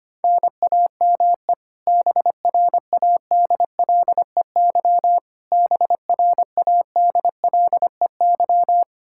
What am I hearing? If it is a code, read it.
Morse code, heard as a keyed tone, about 25 wpm, 700 Hz. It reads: NAME BRADLEY BRADLEY
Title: Practice QSO - Exchange 1 - Beginning - 25wpm